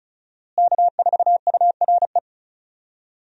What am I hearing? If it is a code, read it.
Morse code, heard as a keyed tone, about 36 wpm, 700 Hz. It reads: K4URE
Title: USA Call Signs - 5 Characters - Repeated 3x - 35wpm